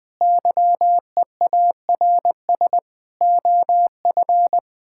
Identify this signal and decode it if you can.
Morse code, heard as a keyed tone, about 20 wpm, 700 Hz. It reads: YEARS OF
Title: Difficult Sentences with Hints - 20wpm